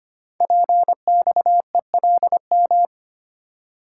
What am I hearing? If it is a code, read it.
Morse code, heard as a keyed tone, about 25 wpm, 700 Hz. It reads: PXELM
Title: Code Groups - 5 Letters - 25wpm